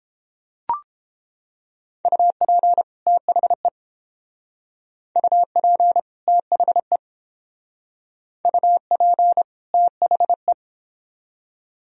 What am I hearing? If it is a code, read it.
Morse code, heard as a keyed tone, about 30 wpm, 700 Hz. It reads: UP THE UP THE UP THE